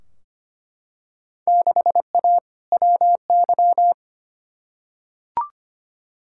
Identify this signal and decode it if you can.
Morse code, heard as a keyed tone, about 25 wpm, 700 Hz. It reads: 6A WY
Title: Field Day Exchanges Encore - 25wpm